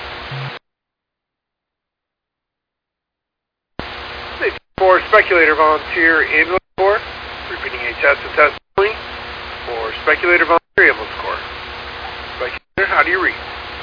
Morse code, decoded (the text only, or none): none